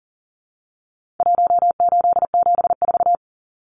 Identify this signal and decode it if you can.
Morse code, heard as a keyed tone, about 40 wpm, 700 Hz. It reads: 1874
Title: Numbers - Contest Style - 4-Digit 40wpm